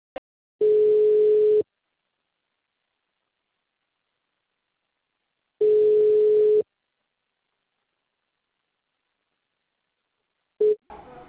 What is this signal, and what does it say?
Ringback tone (Europe, 425 Hz).
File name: out- -2019- - - .63560